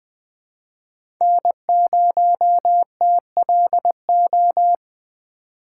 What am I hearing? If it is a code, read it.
Morse code, heard as a keyed tone, about 21 wpm, 700 Hz. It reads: N0TLO
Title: USA Call Signs - 6x Character Spacing - 20wpm